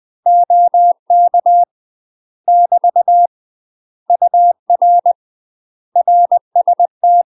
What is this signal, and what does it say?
Morse code, heard as a keyed tone, about 20 wpm, 700 Hz. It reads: OK <BT> UR RST
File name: UK MASTER QSOs TRAINING/20WPM